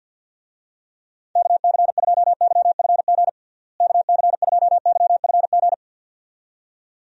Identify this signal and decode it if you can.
Morse code, heard as a keyed tone, about 51 wpm, 700 Hz. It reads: KC2YFG KC2YFG
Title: USA Call Signs - 6 Characters - Repeated 2x - 50wpm